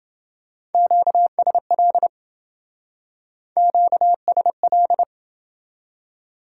Morse code, heard as a keyed tone, about 27 wpm, 700 Hz. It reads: QSL QSL